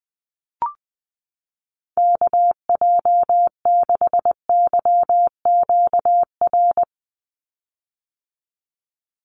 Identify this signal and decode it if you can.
Morse code, heard as a keyed tone, about 20 wpm, 700 Hz. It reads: KJ6YQR
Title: USA Call Signs 20wpm